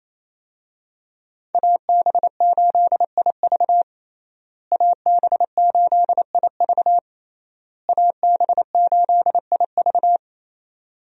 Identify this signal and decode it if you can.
Morse code, heard as a keyed tone, about 28 wpm, 700 Hz. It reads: AB8IV AB8IV AB8IV
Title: USA Call Signs - 5 Characters - Repeated 3x - 28wpm